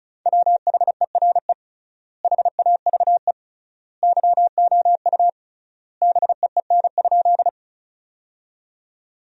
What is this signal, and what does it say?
Morse code, heard as a keyed tone, about 34 wpm, 700 Hz. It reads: WHERE HAVE YOU BEEN?
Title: 3x Word-Spacing - Sentences from Top 200 Words - 35wpm